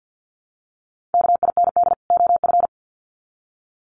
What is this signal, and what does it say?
Morse code, heard as a keyed tone, about 50 wpm, 700 Hz. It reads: KIND OF
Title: Sets of 2 Words 50wpm